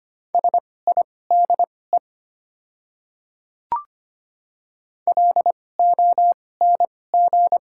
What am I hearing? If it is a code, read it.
Morse code, heard as a keyed tone, about 25 wpm, spaced as for 18 wpm, 700 Hz. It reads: SIDE LONG
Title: 2x Character Spacing - Top 100 Words - 25wpm